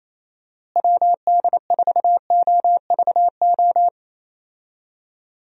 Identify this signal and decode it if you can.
Morse code, heard as a keyed tone, about 28 wpm, 700 Hz. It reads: WD4OVO
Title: USA Call Signs - 6 Characters - Repeated 2x - 28wpm